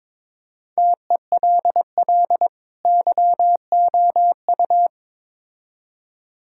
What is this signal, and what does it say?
Morse code, heard as a keyed tone, about 21 wpm, 700 Hz. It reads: TELL YOU
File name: Build A Phrase - 2-Word Phrases - 22wpm